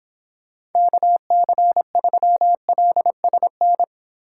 Morse code, heard as a keyed tone, about 30 wpm, 700 Hz. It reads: KC3LSN